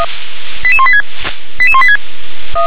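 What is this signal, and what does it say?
Five-tone selective call (ZVEI), heard as quick runs of five tones, tones of about 70 ms. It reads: 88186 88186
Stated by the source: ZVEI five-tone call received by SDR.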